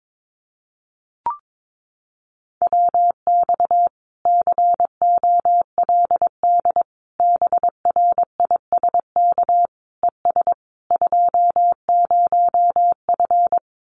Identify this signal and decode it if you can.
Morse code, heard as a keyed tone, about 22 wpm, 700 Hz. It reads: WX COLD BRISK ES 20F
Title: Common QSO Phrases Encore 22wpm